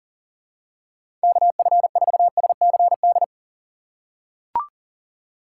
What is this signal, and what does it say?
Morse code, heard as a keyed tone, about 40 wpm, 700 Hz. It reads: KF4SCD
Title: USA Call Signs - 6 Characters - 6x Character Spacing - 40wpm